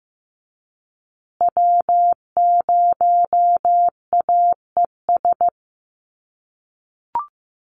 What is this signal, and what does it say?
Morse code, heard as a keyed tone, about 15 wpm, 700 Hz. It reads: W0AES
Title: USA Call Signs Repeated 3x 15wpm